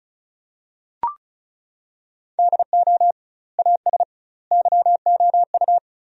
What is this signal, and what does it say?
Morse code, heard as a keyed tone, about 36 wpm, 700 Hz. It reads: DO AS YOU
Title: 2x Word-Spacing - Sentences from Top 100 Words - 35wpm